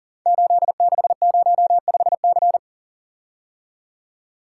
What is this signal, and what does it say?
Morse code, heard as a keyed tone, about 42 wpm, 700 Hz. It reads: 8605C
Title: Code Groups - 5 Letter:Numbers - 40wpm